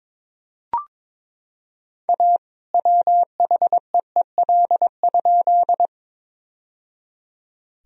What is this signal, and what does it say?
Morse code, heard as a keyed tone, about 22 wpm, 700 Hz. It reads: A WHEEL?
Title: Sentences from Top 1000 Words - Review Part 3 - 22wpm